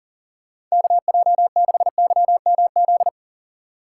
Morse code, heard as a keyed tone, about 40 wpm, 700 Hz. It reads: KJ6YMZ
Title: USA Call Signs - 6 Characters - 8x Character Spacing - 40wpm